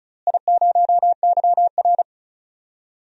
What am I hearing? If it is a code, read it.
Morse code, heard as a keyed tone, about 36 wpm, 700 Hz. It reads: I0YR
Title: DXCC-100 Call Signs - 4 Characters - Repeated 3x - 35wpm